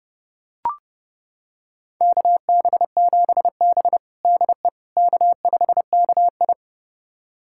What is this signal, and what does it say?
Morse code, heard as a keyed tone, about 32 wpm, 700 Hz. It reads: KB7B DE K5KI
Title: USA Callsign DE Callsign - 4 Characters - 30wpm